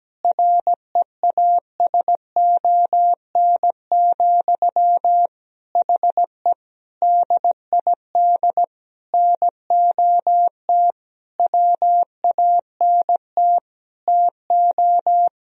Morse code, heard as a keyed tone, about 17 wpm, 700 Hz. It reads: REASON, HE DID NOT WANT TO
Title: Sentences from Top 1000 Words - Review Part 3 - 17wpm